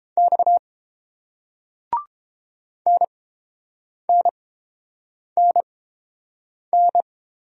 Morse code, heard as a keyed tone, about 30 wpm, spaced as for 7 wpm, 700 Hz. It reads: X NNNN